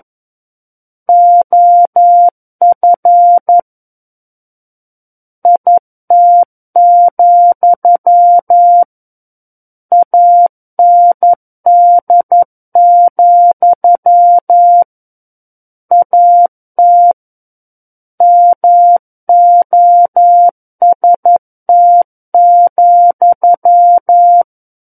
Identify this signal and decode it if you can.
Morse code, heard as a keyed tone, about 11 wpm, 700 Hz. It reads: OF IT, AND, AT MOST,
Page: SKCC - War of the Worlds - 02-Chapter 2 - 11 WPM